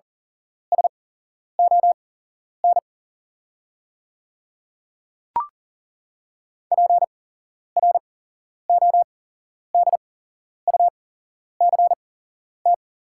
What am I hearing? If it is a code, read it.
Morse code, heard as a keyed tone, about 38 wpm, spaced as for 11 wpm, 690 Hz. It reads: SON PRODUCT
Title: 8x Character Spacing - Top 500 Words - 40wpm